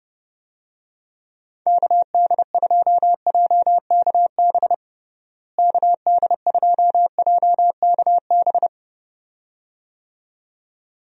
Morse code, heard as a keyed tone, about 30 wpm, 700 Hz. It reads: KD2JKB KD2JKB
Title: USA Call Signs - 6 Characters - Repeated 2x - 30wpm